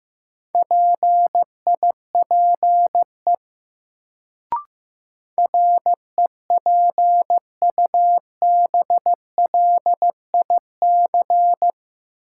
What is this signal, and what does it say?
Morse code, heard as a keyed tone, about 15 wpm, 700 Hz. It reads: PIPE REPUBLIC
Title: Sentences from Top 1500 Words - New Word Drill - 15wpm